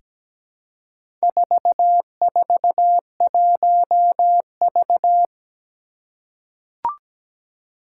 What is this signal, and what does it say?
Morse code, heard as a keyed tone, about 17 wpm, 700 Hz. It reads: 441V
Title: Four Challenging Letter-Number Combinations 17wpm